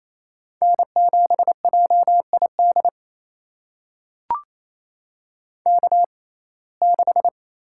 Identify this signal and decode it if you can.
Morse code, heard as a keyed tone, about 28 wpm, spaced as for 20 wpm, 700 Hz. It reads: N7JID K6
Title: USA Call Signs - 5 Characters - 6x Character Spacing - 28wpm